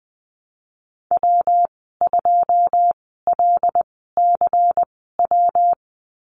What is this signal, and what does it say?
Morse code, heard as a keyed tone, about 19 wpm, spaced as for 14 wpm, 700 Hz. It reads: W2LCW
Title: 179 Lesson 29g - Slash - Call Signs 20wpm